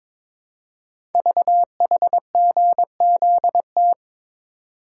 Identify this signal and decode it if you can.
Morse code, heard as a keyed tone, about 22 wpm, 700 Hz. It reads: VHGZT
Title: Code Groups - 5 Letters - 22wpm